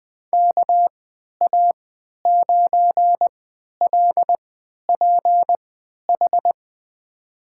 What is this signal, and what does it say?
Morse code, heard as a keyed tone, about 20 wpm, spaced as for 11 wpm, 700 Hz. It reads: KA9LPH
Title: USA Call Signs - 6 Characters - 3x Character Spacing - 20wpm